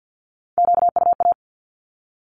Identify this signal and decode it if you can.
Morse code, heard as a keyed tone, about 50 wpm, 700 Hz. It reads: QUA
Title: Q-Codes with Spoken Meaning - 50wpm